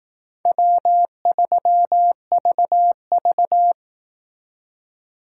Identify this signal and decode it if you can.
Morse code, heard as a keyed tone, about 18 wpm, 700 Hz. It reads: W3VV
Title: Speed Racing - USA 1x2 Call Signs 15wpm